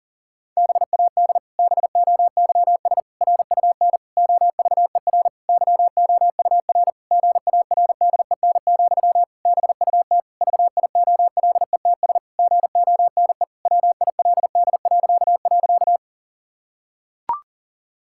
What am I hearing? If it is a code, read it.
Morse code, heard as a keyed tone, about 40 wpm, 700 Hz. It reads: BAD BOYS RUN OVER YOUR GARDEN, BUT VIOLETS GONE WILD..
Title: Pangram Challenge v2 - 40wpm